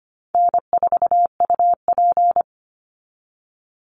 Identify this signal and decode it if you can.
Morse code, heard as a keyed tone, about 25 wpm, 700 Hz. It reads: N4UP
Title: Easy Call Signs Repeated 3x - 25wpm